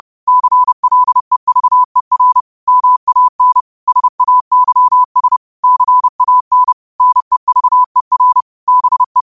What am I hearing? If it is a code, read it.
Morse code, heard as a keyed tone, about 30 wpm, 1000 Hz. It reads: CLEVER MAN SAYS CAN NEVER BE
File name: qotdfiles / 30wpm